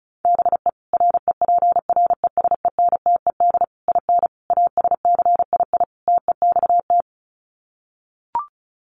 Morse code, heard as a keyed tone, about 34 wpm, 700 Hz. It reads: BE REPRESENTED IN ASCII TEXT
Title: Sets of 5 Words 35wpm